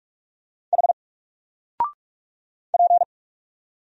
Morse code, heard as a keyed tone, about 44 wpm, 700 Hz. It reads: H P
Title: Single Character (Letter:Number:Punctuation:Prosign) Alt 45wpm